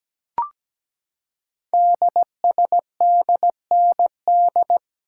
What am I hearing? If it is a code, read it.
Morse code, heard as a keyed tone, about 17 wpm, 700 Hz. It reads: DSDND
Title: Code Groups - 5 Letters - 17wpm